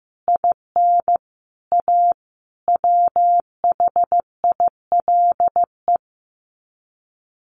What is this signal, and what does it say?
Morse code, heard as a keyed tone, about 15 wpm, 700 Hz. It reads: IN A WHILE
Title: Sets of 5 Words 15wpm